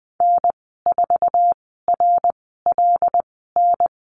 Morse code, heard as a keyed tone, about 19 wpm, spaced as for 14 wpm, 700 Hz. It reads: N4RLN
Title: 128 Lesson 21g - Number 6 - Call Signs 20wpm